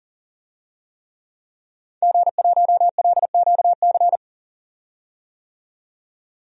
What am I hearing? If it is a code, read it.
Morse code, heard as a keyed tone, about 40 wpm, 700 Hz. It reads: G1LQC
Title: DXCC-100 Call Signs - 5 Characters - 40wpm